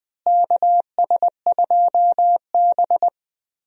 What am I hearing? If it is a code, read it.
Morse code, heard as a keyed tone, about 20 wpm, 700 Hz. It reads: KS2B
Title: USA Call Signs - 4 Characters - Repeated 3x - 20wpm